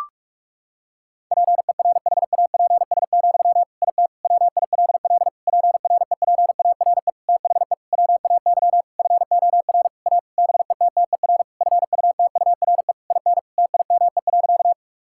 Morse code, heard as a keyed tone, about 46 wpm, 700 Hz. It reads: PERHAPS, IT WILL PREPARE THE WAY FOR A BETTER FUTURE IN TIME.
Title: Sentences from Top 1000 Words - Review Part 6 - 45wpm